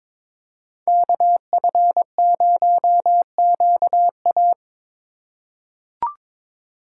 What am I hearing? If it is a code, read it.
Morse code, heard as a keyed tone, about 22 wpm, 700 Hz. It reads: KF0QA
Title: USA Call Signs - 5 Characters - 3x Character Spacing - 22wpm